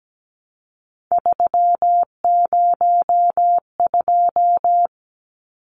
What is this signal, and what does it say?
Morse code, heard as a keyed tone, about 17 wpm, 700 Hz. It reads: 302
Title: Numbers - 3-Digit 17wpm